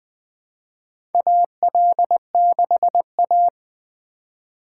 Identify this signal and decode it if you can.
Morse code, heard as a keyed tone, about 20 wpm, 700 Hz. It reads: AL6A
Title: USA Call Signs - 4 Characters - 4x Character Spacing - 20wpm